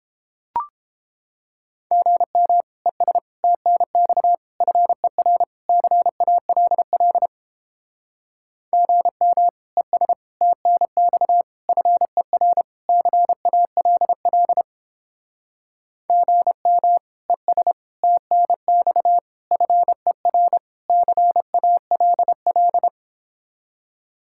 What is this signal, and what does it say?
Morse code, heard as a keyed tone, about 29 wpm, 700 Hz. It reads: GM ES TNX FER CALL GM ES TNX FER CALL GM ES TNX FER CALL